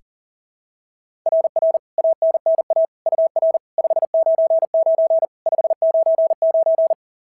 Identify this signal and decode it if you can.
Morse code, heard as a keyed tone, about 40 wpm, 690 Hz. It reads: RR ANNA UR 599 599
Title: SOTA QSO Practice 40wpm